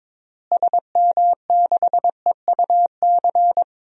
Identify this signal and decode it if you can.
Morse code, heard as a keyed tone, about 22 wpm, 700 Hz. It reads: SM6EUC
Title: DXCC-100 Call Signs - 6 Characters - Repeated 2x - 22wpm